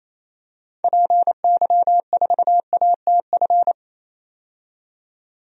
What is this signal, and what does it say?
Morse code, heard as a keyed tone, about 28 wpm, 700 Hz. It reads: PY4ATF
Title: DXCC-100 Call Signs - 6 Characters - 28wpm